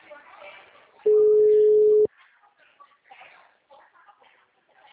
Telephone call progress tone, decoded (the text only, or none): ringback tone